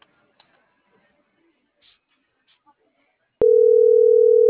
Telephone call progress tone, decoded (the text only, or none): ringback tone